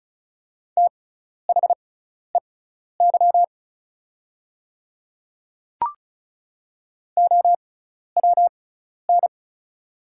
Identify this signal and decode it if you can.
Morse code, heard as a keyed tone, about 34 wpm, spaced as for 12 wpm, 690 Hz. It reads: THEY OWN
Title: 6x Character Spacing - Top 500 Words - 35wpm